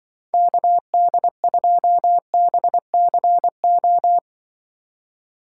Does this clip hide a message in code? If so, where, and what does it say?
Yes: Morse code from 0.3 to 4.2 s, heard as a keyed tone, about 25 wpm, 700 Hz. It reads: KD2BCO